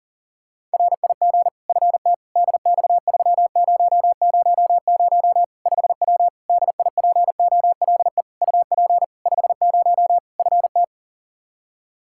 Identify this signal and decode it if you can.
Morse code, heard as a keyed tone, about 37 wpm, 700 Hz. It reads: RIG FT DX3000 5W DIPOLE UP 50 FT
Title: Practice QSO - Exchange 2 - Info - 40wpm